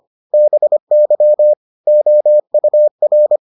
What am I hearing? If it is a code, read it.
Morse code, heard as a keyed tone, about 25 wpm, 600 Hz. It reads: BY OUR